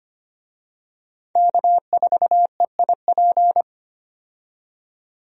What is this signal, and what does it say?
Morse code, heard as a keyed tone, about 25 wpm, 700 Hz. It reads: K4EIP
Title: USA Call Signs - 5 Characters - 25wpm